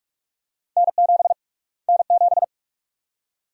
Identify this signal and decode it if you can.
Morse code, heard as a keyed tone, about 42 wpm, 700 Hz. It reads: N7 N7
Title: USA Call Sign Prefixes - Repeated 2x - 45wpm